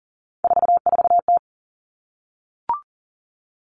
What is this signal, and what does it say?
Morse code, heard as a keyed tone, about 38 wpm, 700 Hz. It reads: 44T
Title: Three Challenging Letter-Number Combinations 40wpm